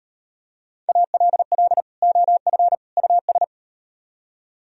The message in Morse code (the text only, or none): ALL OF US